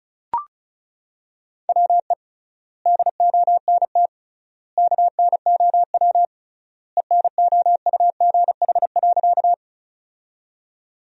Morse code, heard as a keyed tone, about 34 wpm, 700 Hz. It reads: WE DONT KNOW ENOUGH.
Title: 3x Word-Spacing - Sentences from Top 400 Words - 35wpm